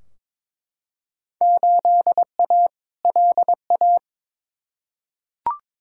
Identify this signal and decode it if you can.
Morse code, heard as a keyed tone, about 22 wpm, 700 Hz. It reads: 8A LA
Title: Field Day Exchanges Encore - 22wpm